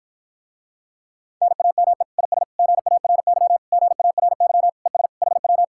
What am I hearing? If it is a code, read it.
Morse code, heard as a keyed tone, about 53 wpm, 700 Hz. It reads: NAME IS GARY GARY ES HW